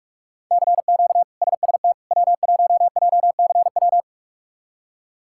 Morse code, heard as a keyed tone, about 41 wpm, 700 Hz. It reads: CQ SST W1JCW
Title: K1USN SST QSO Practice - 45wpm